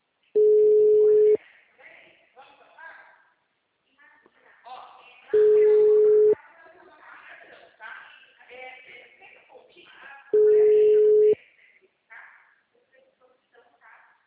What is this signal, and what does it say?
Ringback tone (Europe, 425 Hz), with speech.